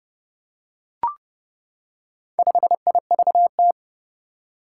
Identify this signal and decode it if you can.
Morse code, heard as a keyed tone, about 31 wpm, 700 Hz. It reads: 5IVT
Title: Four Challenging Letter-Number Combinations 30wpm